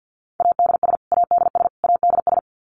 Morse code, heard as a keyed tone, about 50 wpm, 700 Hz. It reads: ADS ADS ADS
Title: Abbreviations Repeated 3x - 50wpm